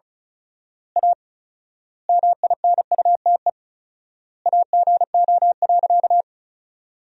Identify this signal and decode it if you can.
Morse code, heard as a keyed tone, about 34 wpm, 700 Hz. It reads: A MINUTE AGO.
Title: 4x Word-Spacing - Sentences from Top 500 Words - 35wpm